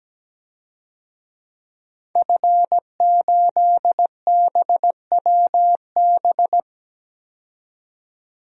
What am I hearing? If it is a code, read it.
Morse code, heard as a keyed tone, about 17 wpm, 700 Hz. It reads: F8BWB
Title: DXCC-100 Call Signs - 5 Characters - 17wpm